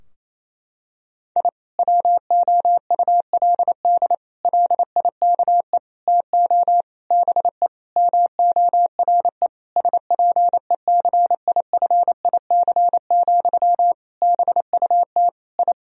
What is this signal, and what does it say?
Morse code, heard as a keyed tone, about 28 wpm, 700 Hz. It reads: I WOULD LIKE TO BE MORE SPECIFIC, BUT I
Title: Sentences from Top 1000 Words - Review Part 3 - 28wpm